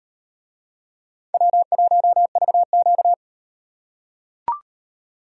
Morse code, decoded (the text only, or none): W1VQ